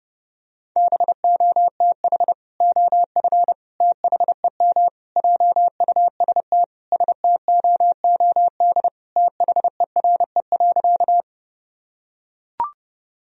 Morse code, heard as a keyed tone, about 30 wpm, 700 Hz. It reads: BOTH OF THEM JUST STOOD THERE.
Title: Sentences from Top 1000 Words - Review Part 2 - 30wpm